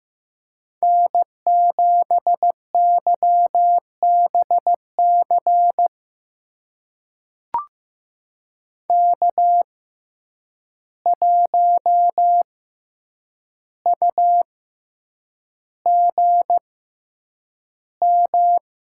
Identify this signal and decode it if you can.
Morse code, heard as a keyed tone, about 15 wpm, spaced as for 8 wpm, 700 Hz. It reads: N7YBC K1UGM
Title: USA Call Signs - 6x Character Spacing - 15wpm